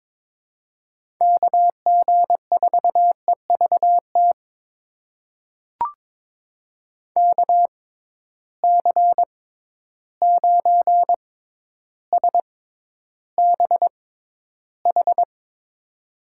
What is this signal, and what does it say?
Morse code, heard as a keyed tone, about 22 wpm, spaced as for 11 wpm, 700 Hz. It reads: KG4EVT KC9SBH
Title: USA Call Signs - 6 Characters - 6x Character Spacing - 22wpm